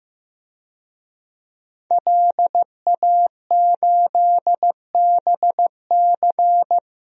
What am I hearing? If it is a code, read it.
Morse code, heard as a keyed tone, about 15 wpm, 700 Hz. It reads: LA8BC
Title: DXCC-100 Call Signs - 5 Characters - Repeated 2x - 15wpm